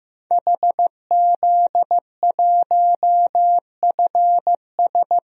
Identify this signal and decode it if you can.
Morse code, heard as a keyed tone, about 15 wpm, 700 Hz. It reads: HZ1FS
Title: DXCC-100 Call Signs - 5 Characters - 15wpm